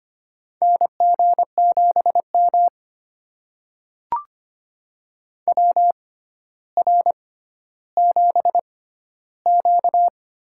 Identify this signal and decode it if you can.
Morse code, heard as a keyed tone, about 25 wpm, spaced as for 13 wpm, 700 Hz. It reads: NG7M WR7Q
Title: USA Call Signs - 4 Characters - 6x Character Spacing - 25wpm